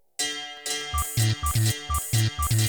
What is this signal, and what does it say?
There is no code.